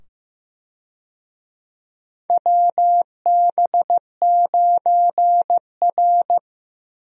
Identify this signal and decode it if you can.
Morse code, heard as a keyed tone, about 15 wpm, 700 Hz. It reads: WB9R
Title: USA Call Signs - 4 Characters - 15wpm